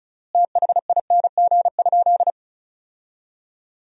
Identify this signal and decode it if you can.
Morse code, heard as a keyed tone, about 34 wpm, 700 Hz. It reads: THING?
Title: 4x Word-Spacing - Sentences from Top 200 Words - 35wpm